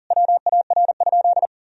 Morse code, heard as a keyed tone, about 40 wpm, 700 Hz. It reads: WAR?
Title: Sentences from Top 1000 Words - Review Part 2 - 40wpm